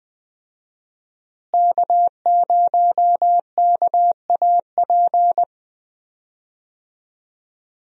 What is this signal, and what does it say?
Morse code, heard as a keyed tone, about 20 wpm, 700 Hz. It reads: K0KAP
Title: USA Call Signs - 5 Characters - Repeated 3x - 20wpm